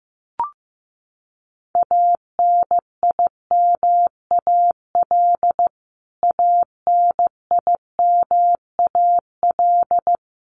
Morse code, heard as a keyed tone, about 15 wpm, 700 Hz. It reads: ANIMAL ANIMAL
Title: Top 500 Words Repeated 15wpm